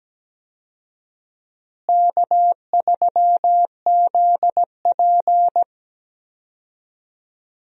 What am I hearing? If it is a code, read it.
Morse code, heard as a keyed tone, about 17 wpm, 700 Hz. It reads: K3ZP
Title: USA Call Signs - 4 Characters - Repeated 3x - 17wpm